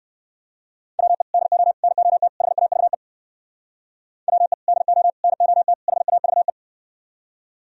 Morse code, heard as a keyed tone, about 68 wpm, 700 Hz. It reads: WE DO NOT HAVE WE DO NOT HAVE